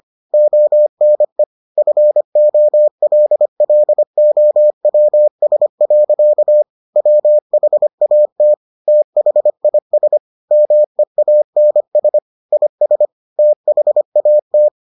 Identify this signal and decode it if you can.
Morse code, heard as a keyed tone, about 25 wpm, 600 Hz. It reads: ONE FOLLOWS. WHAT THIS MEANS IS THAT